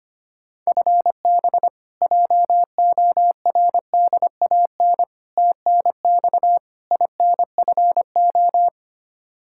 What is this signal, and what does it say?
Morse code, heard as a keyed tone, about 25 wpm, 700 Hz. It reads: FB JORDAN TNX INFO
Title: Practice QSO - Exchange 3 - More Info - 25wpm